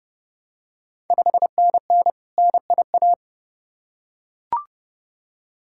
Morse code, heard as a keyed tone, about 32 wpm, 690 Hz. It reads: 5NN NIA
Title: Ontario QSO Party Exchange 30wpm